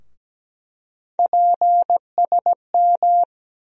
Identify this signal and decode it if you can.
Morse code, heard as a keyed tone, about 17 wpm, 700 Hz. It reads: PSM
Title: USA Call Sign Postfixes - 3 Letters 17wpm